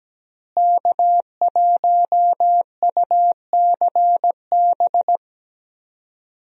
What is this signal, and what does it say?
Morse code, heard as a keyed tone, about 17 wpm, 700 Hz. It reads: K1UCB
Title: USA Call Signs - 5 Characters - Repeated 2x - 17wpm